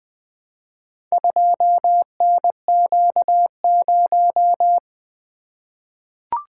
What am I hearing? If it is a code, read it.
Morse code, heard as a keyed tone, about 20 wpm, 700 Hz. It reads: 2NQ0
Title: Four Letter-Number Combinations - 20wpm